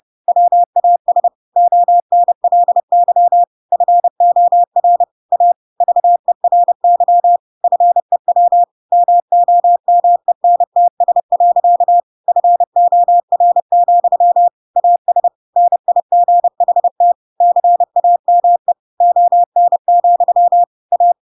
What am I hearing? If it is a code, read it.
Morse code, heard as a keyed tone, about 30 wpm, 700 Hz. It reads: WAS ONLY FOR A VERY FEW MOMENTS. FOR, AS NIGHT CAME ON, A